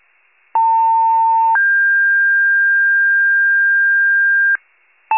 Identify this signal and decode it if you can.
Two-tone page, heard as a one-second tone, then a three-second tone, about 900 then 1600 Hz, with noise. It is heard from 0.5 to 4.6 s.